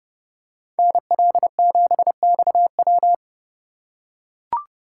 Morse code, heard as a keyed tone, about 28 wpm, 700 Hz. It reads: NL7XW